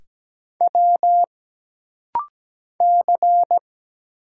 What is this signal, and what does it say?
Morse code, heard as a keyed tone, about 17 wpm, 700 Hz. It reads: W C
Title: Single Letter-Number Alt - Rapid-Fire 17wpm